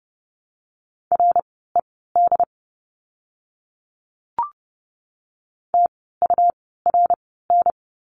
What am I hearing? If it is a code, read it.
Morse code, heard as a keyed tone, about 30 wpm, spaced as for 17 wpm, 700 Hz. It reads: RED TURN
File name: Lesson 17f - Letter u - Words 30wpm